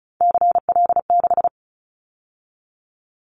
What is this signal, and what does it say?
Morse code, heard as a keyed tone, about 36 wpm, 700 Hz. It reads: CL6
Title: Callsign Prefixes for Top 100 DXCC - 35wpm